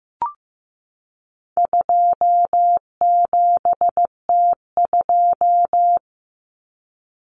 Morse code, heard as a keyed tone, about 15 wpm, 700 Hz. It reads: 27T2
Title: Common Cut Numbers - 4-Digit 15wpm